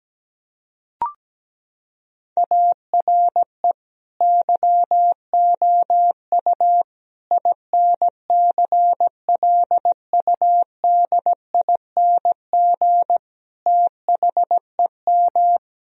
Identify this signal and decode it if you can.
Morse code, heard as a keyed tone, about 17 wpm, 700 Hz. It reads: ARE YOU INCLUDING THEM
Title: Sentences from Top 1000 Words - Review Part 4 - 17wpm